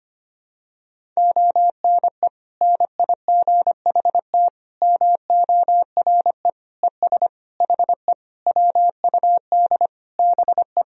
Morse code, heard as a keyed tone, about 25 wpm, 700 Hz. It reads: ONE NIGHT MORE ES HE WUD BE
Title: Sentences from Top 300 Words and Abbreviations 25wpm